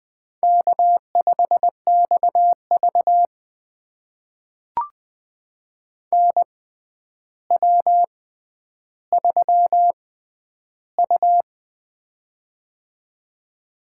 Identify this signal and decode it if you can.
Morse code, heard as a keyed tone, about 20 wpm, spaced as for 10 wpm, 700 Hz. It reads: K5XV NW3U
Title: USA Call Signs - 4 Characters - 6x Character Spacing - 20wpm